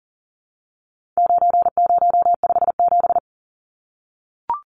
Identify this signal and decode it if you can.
Morse code, heard as a keyed tone, about 40 wpm, 700 Hz. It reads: 9057
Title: Numbers - 4-Digit 40wpm